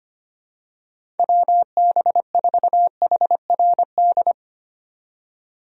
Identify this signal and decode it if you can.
Morse code, heard as a keyed tone, about 25 wpm, 700 Hz. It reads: WB4HRD
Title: USA Call Signs - 6 Characters - 25wpm